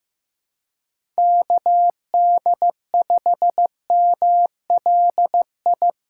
Morse code, heard as a keyed tone, about 15 wpm, 700 Hz. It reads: KD5MLI
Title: USA Call Signs - 8x Character Spacing - 15wpm